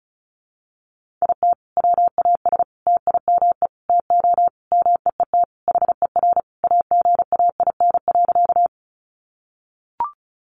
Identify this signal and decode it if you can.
Morse code, heard as a keyed tone, about 34 wpm, 700 Hz. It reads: IT WAS TIME TO MEET HER AGAIN.
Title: Sentences from Top 1100 Words 35wpm